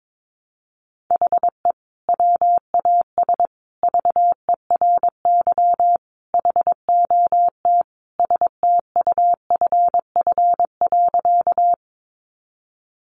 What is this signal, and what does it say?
Morse code, heard as a keyed tone, about 22 wpm, 700 Hz. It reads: HE WAS VERY HOT STUFF.
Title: Sentences from Top 1400 Words 22wpm